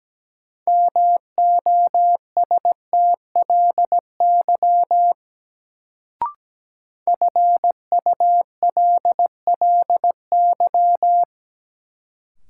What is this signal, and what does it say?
Morse code, heard as a keyed tone, about 17 wpm, 700 Hz. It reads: MOSTLY FULLY
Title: Sentences from Top 1300 Words - New Word Drill Alt - 17wpm